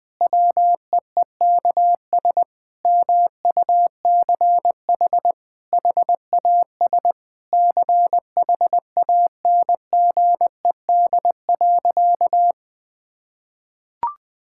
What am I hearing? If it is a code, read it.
Morse code, heard as a keyed tone, about 20 wpm, 700 Hz. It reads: WEEKS MUCH HAS CHANGED.
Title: Sentences with 5-Word Phrases - 20wpm